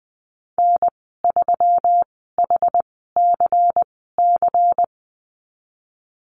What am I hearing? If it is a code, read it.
Morse code, heard as a keyed tone, about 20 wpm, spaced as for 14 wpm, 700 Hz. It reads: N3HCC
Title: 161 Lesson 26g - Letter g - Call Signs 20wpm